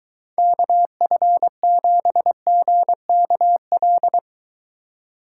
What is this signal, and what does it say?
Morse code, heard as a keyed tone, about 23 wpm, 700 Hz. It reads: KF7GKL